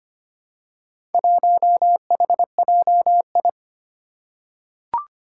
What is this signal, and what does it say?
Morse code, heard as a keyed tone, about 25 wpm, 700 Hz. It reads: 1HJI
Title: Four Challenging Letter-Number Combinations 25wpm